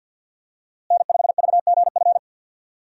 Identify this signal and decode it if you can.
Morse code, heard as a keyed tone, about 51 wpm, 700 Hz. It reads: N5VGF
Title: USA Call Signs - 5 Characters - Repeated 3x - 50wpm